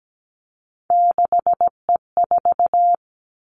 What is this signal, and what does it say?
Morse code, heard as a keyed tone, about 17 wpm, 700 Hz. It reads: 6E4
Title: Three Challenging Letter-Number Combinations 17wpm